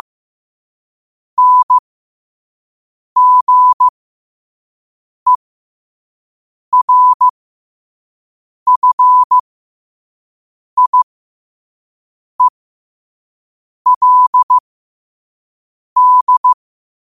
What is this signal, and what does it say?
Morse code, heard as a keyed tone, about 15 wpm, spaced as for 5 wpm, 1000 Hz. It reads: NGERFIELD
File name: qotdfiles / 5wpm